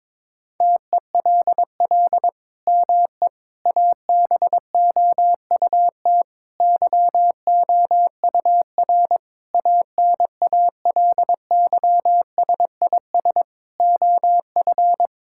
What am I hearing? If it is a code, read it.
Morse code, heard as a keyed tone, about 22 wpm, 700 Hz. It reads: TELL ME ABOUT YOUR ANALYSIS OF
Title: Sentences from Top 1000 Words - Review Part 5 - 22wpm